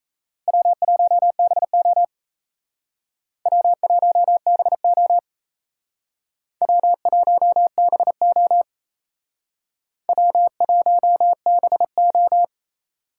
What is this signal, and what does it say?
Morse code, heard as a keyed tone, about 35 wpm, 700 Hz. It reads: W1BO W1BO W1BO W1BO
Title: Speed Racing - USA Call Signs 28wpm